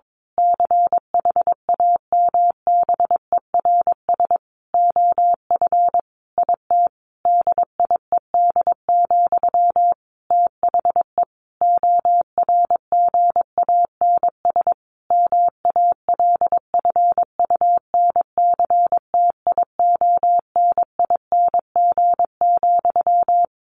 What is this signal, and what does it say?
Morse code, heard as a keyed tone, about 22 wpm, 700 Hz. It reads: CHAMBERS OF IT DIED, THE ORGANS MALFUNCTIONING,
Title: A Sound Of Thunder 22wpm